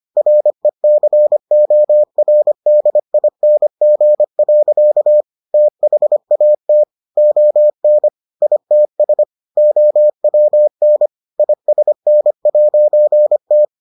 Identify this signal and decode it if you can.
Morse code, heard as a keyed tone, about 25 wpm, 600 Hz. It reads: RECORDING. THAT ON ITS OWN ISN'T